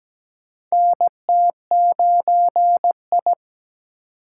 Morse code, heard as a keyed tone, about 17 wpm, 700 Hz. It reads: NT9I
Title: USA Call Signs - 2x Character Spacing - 17wpm